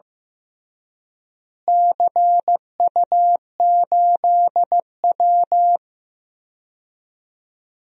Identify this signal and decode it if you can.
Morse code, heard as a keyed tone, about 15 wpm, 700 Hz. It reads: CU8W
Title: DXCC-100 Call Signs - 4 Characters - Repeated 3x - 15wpm